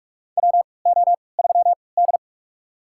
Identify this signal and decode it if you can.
Morse code, heard as a keyed tone, about 45 wpm, spaced as for 25 wpm, 700 Hz. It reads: WO3D
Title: USA Call Signs - 4 Characters - 3x Character Spacing - 45wpm